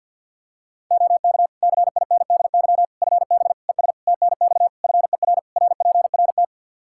Morse code, heard as a keyed tone, about 48 wpm, 700 Hz. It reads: OK CINDY FB ES TNX FER RPRT
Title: K1USN SST QSO Practice 50wpm